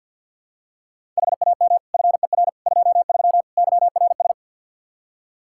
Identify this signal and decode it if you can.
Morse code, heard as a keyed tone, about 52 wpm, 700 Hz. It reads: HAM FER 23 YRS
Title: Practice QSO - Exchange 3 - More Info - 50wpm